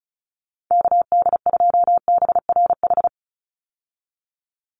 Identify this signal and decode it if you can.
Morse code, heard as a keyed tone, about 34 wpm, 700 Hz. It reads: KD2BRH
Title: USA Call Signs 35wpm